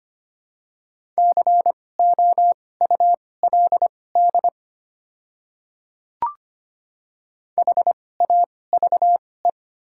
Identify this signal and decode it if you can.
Morse code, heard as a keyed tone, about 25 wpm, spaced as for 18 wpm, 700 Hz. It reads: COULD HAVE
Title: 2x Character Spacing - Top 200 Words - 25wpm